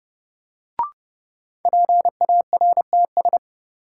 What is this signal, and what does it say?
Morse code, heard as a keyed tone, about 30 wpm, 700 Hz. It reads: PARTS
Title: Sentences from Top 1000 Words - Review Part 6 - New Word Drill - 30wpm